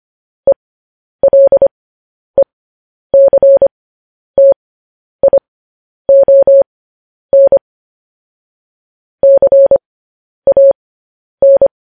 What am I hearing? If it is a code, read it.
Morse code, heard as a keyed tone, about 25 wpm, spaced as for 10 wpm, 550 Hz. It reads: ELECTION CAN
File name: podcast / News headlines-1x-25x10